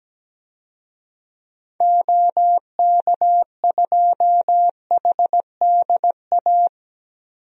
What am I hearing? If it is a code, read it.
Morse code, heard as a keyed tone, about 17 wpm, 700 Hz. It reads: OK2HDA
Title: DXCC-100 Call Signs - 6 Characters - Repeated 3x - 17wpm